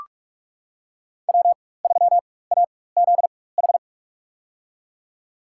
Morse code, heard as a keyed tone, about 45 wpm, spaced as for 21 wpm, 700 Hz. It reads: W3AZH
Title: USA Call Signs - 5 Characters - 4x Character Spacing - 45wpm